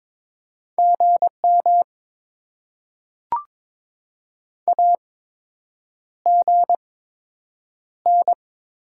Morse code, heard as a keyed tone, about 22 wpm, spaced as for 8 wpm, 700 Hz. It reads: GM AGN
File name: Top 20 QSO Elements - 8x Character Spacing - 22wpm